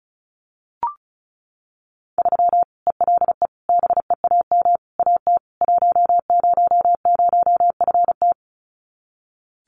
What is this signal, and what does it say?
Morse code, heard as a keyed tone, about 34 wpm, 700 Hz. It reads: 3 ELE BEAM AT 100FT
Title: Common QSO Phrases 35wpm